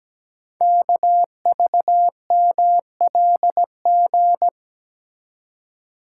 Morse code, heard as a keyed tone, about 17 wpm, 700 Hz. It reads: KVMLG
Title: Code Groups - 5 Letters - 17wpm